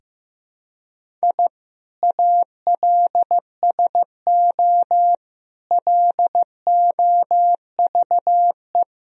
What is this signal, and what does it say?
Morse code, heard as a keyed tone, about 15 wpm, 700 Hz. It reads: I ALSO LOVE
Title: Sentences from Top 1000 Words - Review Part 1 - 15wpm